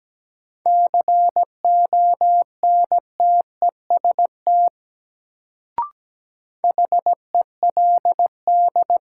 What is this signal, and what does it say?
Morse code, heard as a keyed tone, about 17 wpm, 700 Hz. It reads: CONTEST HELD
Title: Sentences from Top 1600 Words - New Word Drill - 17wpm